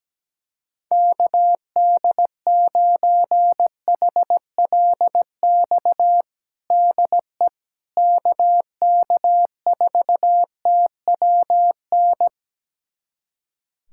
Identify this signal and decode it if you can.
Morse code, heard as a keyed tone, about 17 wpm, 700 Hz. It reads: KD9HLX DE KK4TWN
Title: USA Callsign DE Callsign - 6 Characters - 17wpm